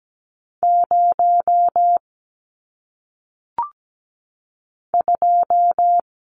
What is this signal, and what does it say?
Morse code, heard as a keyed tone, about 17 wpm, 700 Hz. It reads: 0 2
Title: Numbers - 1-Digit 17wpm